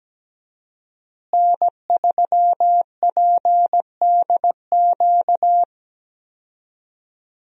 Morse code, heard as a keyed tone, about 17 wpm, 700 Hz. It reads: N3PDQ
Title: USA Call Signs - 5 Characters - Repeated 2x - 17wpm